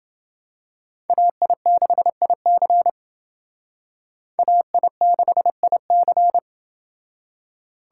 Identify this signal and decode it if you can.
Morse code, heard as a keyed tone, about 28 wpm, 700 Hz. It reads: AI6IC AI6IC